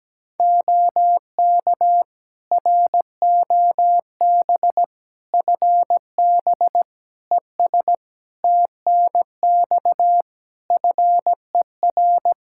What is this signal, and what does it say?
Morse code, heard as a keyed tone, about 17 wpm, 700 Hz. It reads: OK ROB FB ES TNX FER
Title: Practice QSO - Exchange 2 - Info - 17wpm